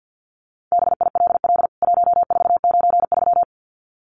Morse code, heard as a keyed tone, about 50 wpm, 700 Hz. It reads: BILL 1493
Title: CWT Exchange 50wpm